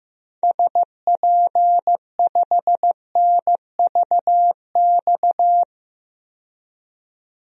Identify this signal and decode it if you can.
Morse code, heard as a keyed tone, about 15 wpm, 700 Hz. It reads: SP5NVX
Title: DXCC-100 Call Signs - 6 Characters - 15wpm